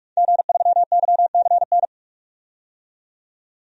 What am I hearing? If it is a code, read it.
Morse code, heard as a keyed tone, about 46 wpm, 700 Hz. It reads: G3YCN
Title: DXCC-100 Call Signs - 5 Characters - 45wpm